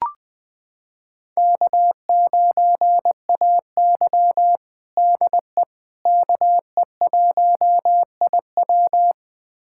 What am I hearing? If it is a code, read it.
Morse code, heard as a keyed tone, about 21 wpm, 700 Hz. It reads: K9AY DE KE1IW
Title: USA Callsign DE Callsign 20wpm